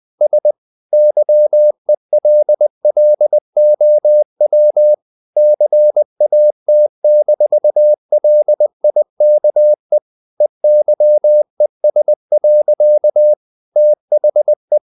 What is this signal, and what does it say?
Morse code, heard as a keyed tone, about 20 wpm, 600 Hz. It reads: S YELLOW CAT-LIKE EYES. THE
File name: the old man and the sea part 2 20wpm